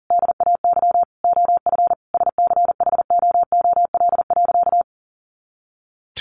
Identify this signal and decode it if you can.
Morse code, heard as a keyed tone, about 40 wpm, 700 Hz. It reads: DAY OF SCHOOL.
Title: Additional Easy Phrases 40wpm